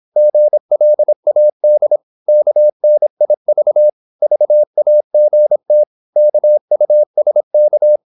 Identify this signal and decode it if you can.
Morse code, heard as a keyed tone, about 26 wpm, 600 Hz. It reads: GLAD KNIV VAGT KUSK
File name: Korte ord DK 26wpm